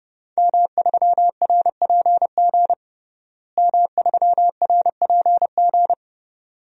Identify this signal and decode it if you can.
Morse code, heard as a keyed tone, about 32 wpm, 700 Hz. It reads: M3RPG M3RPG
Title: DXCC-100 Call Signs - 5 Characters - Repeated 2x - 30wpm